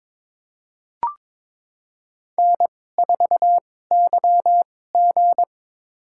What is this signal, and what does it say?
Morse code, heard as a keyed tone, about 22 wpm, spaced as for 16 wpm, 700 Hz. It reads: N4YG
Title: USA Call Signs - 4 Characters - 2x Character Spacing - 22wpm